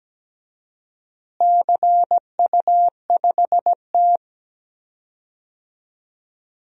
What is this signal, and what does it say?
Morse code, heard as a keyed tone, about 17 wpm, 700 Hz. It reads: CU5T
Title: DXCC-100 Call Signs - 4 Characters - Repeated 2x - 17wpm